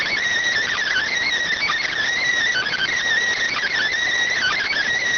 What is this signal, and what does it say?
Emergency Alert System (SAME) tones, heard as harsh buzzing bursts. It is heard from 0.8 to 4.8 s.